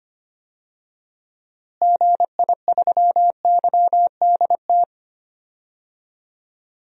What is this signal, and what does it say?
Morse code, heard as a keyed tone, about 25 wpm, 700 Hz. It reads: GI3YDT
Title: DXCC-100 Call Signs - 6 Characters - Repeated 2x - 25wpm